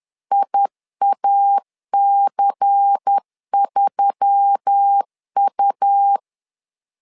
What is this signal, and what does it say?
Morse code, heard as a keyed tone, about 10 wpm, 800 Hz. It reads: IAC3U
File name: LEZIONI / 5 LBVC / 5 LBVC 10wpm 2349